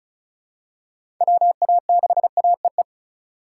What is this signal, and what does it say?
Morse code, heard as a keyed tone, about 34 wpm, 700 Hz. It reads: WA6AEE
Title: K1USN SST QSO Practice - 35wpm